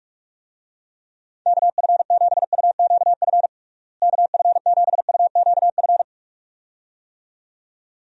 Morse code, heard as a keyed tone, about 46 wpm, 700 Hz. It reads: KF7UQF KF7UQF
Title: USA Call Signs - 6 Characters - Repeated 2x - 45wpm